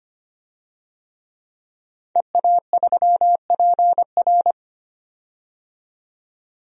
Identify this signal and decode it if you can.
Morse code, heard as a keyed tone, about 25 wpm, 700 Hz. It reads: EA3PR
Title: DXCC-100 Call Signs - 5 Characters - 25wpm